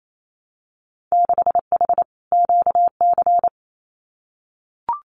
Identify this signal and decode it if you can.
Morse code, heard as a keyed tone, about 28 wpm, 700 Hz. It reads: 6H QC
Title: Winter Field Day Exchanges Encore 28wpm